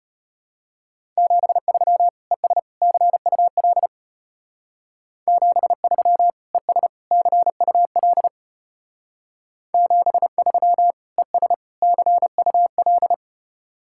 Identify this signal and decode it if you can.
Morse code, heard as a keyed tone, about 32 wpm, 700 Hz. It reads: 73 ES CUL 73 ES CUL 73 ES CUL